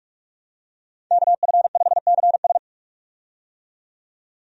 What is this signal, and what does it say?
Morse code, heard as a keyed tone, about 46 wpm, 700 Hz. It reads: KF5CS
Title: USA Call Signs - 5 Characters - 45wpm